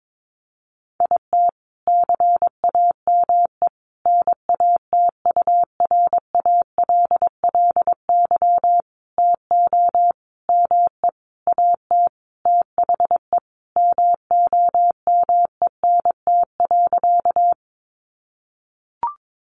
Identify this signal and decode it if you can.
Morse code, heard as a keyed tone, about 22 wpm, 700 Hz. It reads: IT CAME NATURALLY TO ME AT THE MOMENT.
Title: Sentences from Top 1300 Words 22wpm